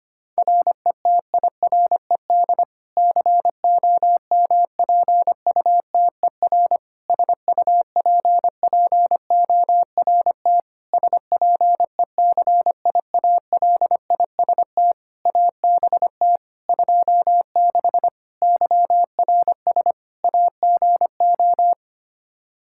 Morse code, heard as a keyed tone, about 25 wpm, 700 Hz. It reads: RETIRED COMPUTER SUPPORT SPECIALIST ABT 26 YRS AGO
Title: Practice QSO - Exchange 3 - More Info - 25wpm